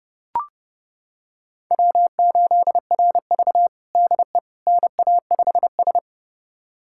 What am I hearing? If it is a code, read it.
Morse code, heard as a keyed tone, about 30 wpm, 700 Hz. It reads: W8RV DE NA5S
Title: USA Callsign DE Callsign - 4 Characters - 30wpm